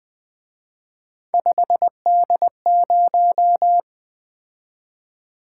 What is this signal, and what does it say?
Morse code, heard as a keyed tone, about 24 wpm, 700 Hz. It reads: 5D0
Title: Speed-Racing - Three Letter-Number Combinations - 20wpm